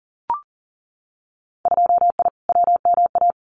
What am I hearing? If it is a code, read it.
Morse code, heard as a keyed tone, about 40 wpm, 700 Hz. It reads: 2I WMA
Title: Winter Field Day Exchanges Encore 40wpm